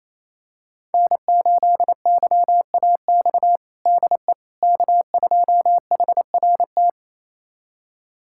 Morse code, heard as a keyed tone, about 28 wpm, 690 Hz. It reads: N8YAX DE K2HRT
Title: USA Callsign DE Callsign - 5 Characters - 28wpm